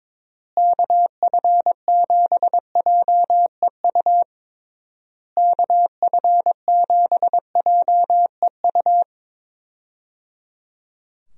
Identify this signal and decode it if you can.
Morse code, heard as a keyed tone, about 22 wpm, 700 Hz. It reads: KF7JEU KF7JEU
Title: USA Call Signs - 6 Characters - Repeated 3x - 22wpm